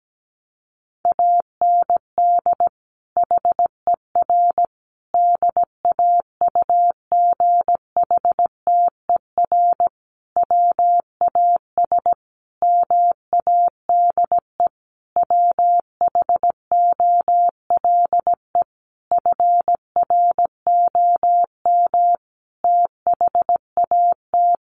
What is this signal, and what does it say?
Morse code, heard as a keyed tone, about 17 wpm, 700 Hz. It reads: AND HER DAUGHTER WAS MADE WHOLE FROM THAT
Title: Sentences from Top 1500 Words 17wpm